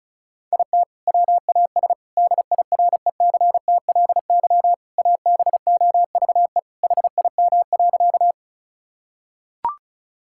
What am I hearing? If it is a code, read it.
Morse code, heard as a keyed tone, about 36 wpm, 700 Hz. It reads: IT WAS DIRECTLY ABOVE HIM.
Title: Sentences from Top 1000 Words - Review Part 5 - 35wpm